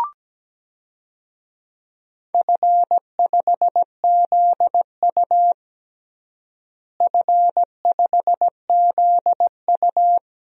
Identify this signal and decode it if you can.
Morse code, heard as a keyed tone, about 17 wpm, 700 Hz. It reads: F5ZU F5ZU
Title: DXCC-100 Call Signs - 4 Characters - Repeated 3x - 17wpm